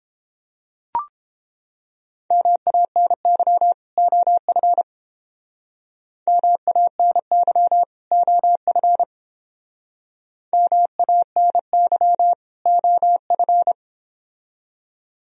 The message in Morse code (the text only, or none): MANY OF MANY OF MANY OF